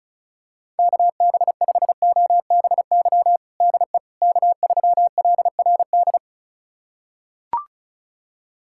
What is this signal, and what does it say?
Morse code, heard as a keyed tone, about 36 wpm, 700 Hz. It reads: KB5OBY DE K3LRD
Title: USA Callsign DE Callsign 35wpm